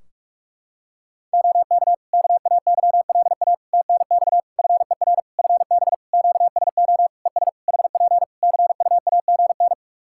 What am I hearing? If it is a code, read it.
Morse code, heard as a keyed tone, about 46 wpm, 700 Hz. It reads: OK KAYLA TNX FER FB QSO ES HP CUAGN
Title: K1USN SST QSO Practice 45wpm